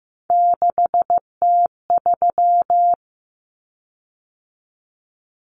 Morse code, heard as a keyed tone, about 15 wpm, 700 Hz. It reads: 6T3
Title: Common Cut Numbers - 3-Digit 15wpm